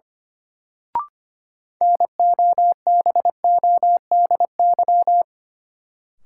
Morse code, heard as a keyed tone, about 25 wpm, 700 Hz. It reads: NOBODY
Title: Sentences from Top 1600 Words - New Word Drill Alt - 25wpm